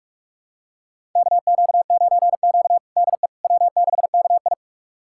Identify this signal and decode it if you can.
Morse code, heard as a keyed tone, about 44 wpm, 690 Hz. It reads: KQ9Q DE W6KI
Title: USA Callsign DE Callsign - 4 Characters - 45wpm